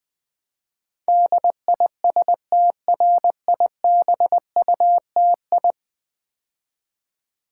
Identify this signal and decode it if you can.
Morse code, heard as a keyed tone, about 20 wpm, 700 Hz. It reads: DISTRIBUTI
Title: Build A Word - 5 Letters At A Time - 12-Letter Words - 20wpm